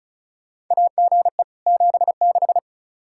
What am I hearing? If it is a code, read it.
Morse code, heard as a keyed tone, about 34 wpm, 700 Hz. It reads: AGE 76
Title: Practice QSO - Exchange 3 - More Info - 35wpm